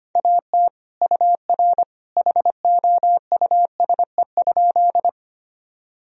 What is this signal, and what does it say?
Morse code, heard as a keyed tone, about 25 wpm, 700 Hz. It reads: AT UR HOUSE?
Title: Sentences from Top 200 Words and Abbreviations 25wpm